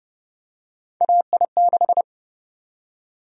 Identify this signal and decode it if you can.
Morse code, heard as a keyed tone, about 30 wpm, 700 Hz. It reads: AI6
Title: USA Call Sign Prefixes - Repeated 2x - 30wpm